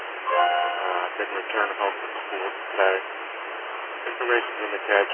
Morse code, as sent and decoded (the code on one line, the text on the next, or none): none